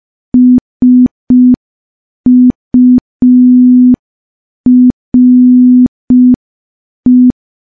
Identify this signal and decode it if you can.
Morse code, heard as a keyed tone, about 5 wpm, 250 Hz. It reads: SURE
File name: morse / morse-training-05WPM